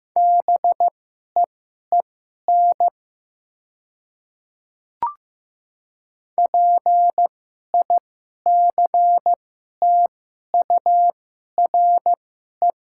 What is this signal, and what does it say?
Morse code, heard as a keyed tone, about 15 wpm, spaced as for 11 wpm, 700 Hz. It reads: BEEN PICTURE
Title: 2x Character Spacing - Top 500 Words - 15wpm